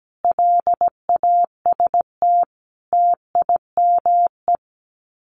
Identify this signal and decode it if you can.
Morse code, heard as a keyed tone, about 17 wpm, 700 Hz. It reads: LAST TIME
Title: Sets of 5 Words Encore 17wpm